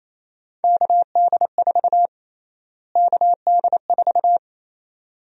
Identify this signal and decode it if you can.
Morse code, heard as a keyed tone, about 28 wpm, 700 Hz. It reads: KD4 KD4
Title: USA Call Sign Prefixes - Repeated 2x - 28wpm